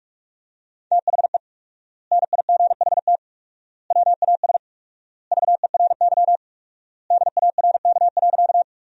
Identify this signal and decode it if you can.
Morse code, heard as a keyed tone, about 46 wpm, 700 Hz. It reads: THE NIGHT WAS VERY DARK.
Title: 4x Word-Spacing - Sentences from Top 500 Words - 45wpm